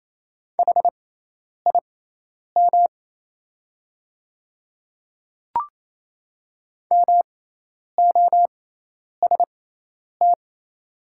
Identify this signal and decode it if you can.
Morse code, heard as a keyed tone, about 28 wpm, spaced as for 10 wpm, 690 Hz. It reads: HIM MOST
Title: 6x Character Spacing - Top 100 Words - 28wpm